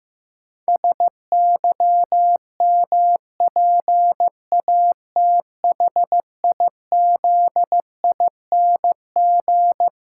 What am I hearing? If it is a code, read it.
Morse code, heard as a keyed tone, about 16 wpm, 700 Hz. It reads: SYMPATHIZING
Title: Pangram Challenge v2 - 15wpm